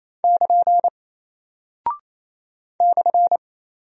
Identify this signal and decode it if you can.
Morse code, heard as a keyed tone, about 28 wpm, 700 Hz. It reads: <KN> /
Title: Single Character (Letter:Number:Punctuation:Prosign) Alt 28wpm